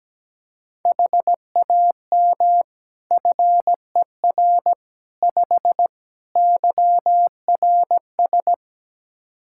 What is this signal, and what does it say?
Morse code, heard as a keyed tone, about 17 wpm, 700 Hz. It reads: HAM FER 5 YRS
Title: Practice QSO - Exchange 3 - More Info - 17wpm